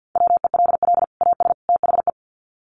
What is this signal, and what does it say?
Morse code, heard as a keyed tone, about 50 wpm, 700 Hz. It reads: WELL AS THE
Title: Sets of 3 Words 50wpm